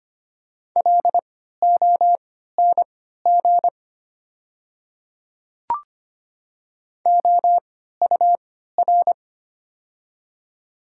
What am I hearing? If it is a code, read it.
Morse code, heard as a keyed tone, about 25 wpm, spaced as for 14 wpm, 690 Hz. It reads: LONG OUR
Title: 3x Character Spacing - Top 200 Words - 25wpm